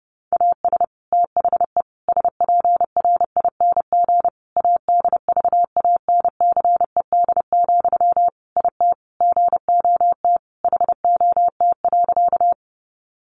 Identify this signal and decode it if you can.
Morse code, heard as a keyed tone, about 30 wpm, 700 Hz. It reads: AS THE SPRING ADVANCED, IT GOT HOT.
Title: Sentences from Top 1200 Words 30wpm